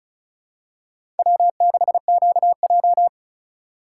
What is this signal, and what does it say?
Morse code, heard as a keyed tone, about 36 wpm, 700 Hz. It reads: W6QJ
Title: USA Call Signs - 4 Characters - 3x Character Spacing - 35wpm